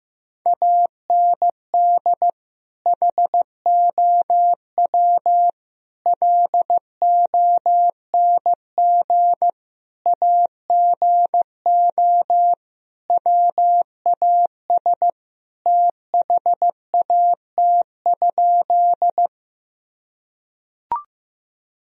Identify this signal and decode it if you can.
Morse code, heard as a keyed tone, about 15 wpm, 700 Hz. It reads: AND HOW LONG AGO WAS THAT?
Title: Sentences from Top 1000 Words - Review Part 3 - 15wpm